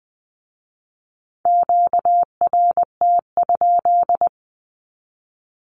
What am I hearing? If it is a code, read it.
Morse code, heard as a keyed tone, about 20 wpm, 700 Hz. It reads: QRT?
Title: Q-Codes with Spoken Meaning - 20wpm